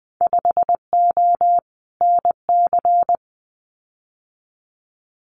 Morse code, heard as a keyed tone, about 20 wpm, 700 Hz. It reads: 5O NC
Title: Winter Field Day Exchanges 20wpm